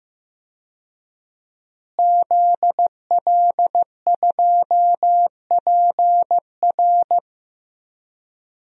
Morse code, heard as a keyed tone, about 15 wpm, 700 Hz. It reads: ZL2PR
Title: DXCC-100 Call Signs - 5 Characters - Repeated 3x - 15wpm